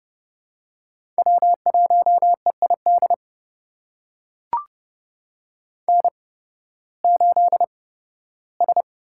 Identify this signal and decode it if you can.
Morse code, heard as a keyed tone, about 31 wpm, spaced as for 16 wpm, 700 Hz. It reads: W1EID N8S
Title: USA Call Signs - 8x Character Spacing - 30wpm